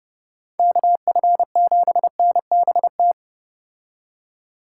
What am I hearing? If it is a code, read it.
Morse code, heard as a keyed tone, about 29 wpm, 700 Hz. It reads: KF7NBT